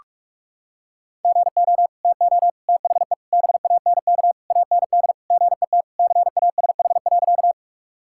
Morse code, heard as a keyed tone, about 41 wpm, 690 Hz. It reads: GO TO THE BANK AND GET CASH.
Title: Sentences from Top 1000 Words - Review Part 5 - 45wpm